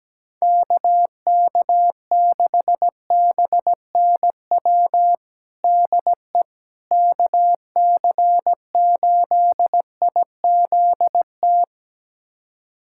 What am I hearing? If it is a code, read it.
Morse code, heard as a keyed tone, about 17 wpm, 700 Hz. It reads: KK6BNW DE KC8IZT
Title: USA Callsign DE Callsign 17wpm